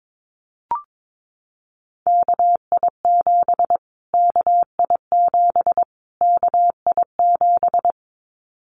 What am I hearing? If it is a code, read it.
Morse code, heard as a keyed tone, about 22 wpm, 700 Hz. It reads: KI7 KI7 KI7
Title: Callsign Prefixes for Top 25 DXCC - Repeated 3x - 22wpm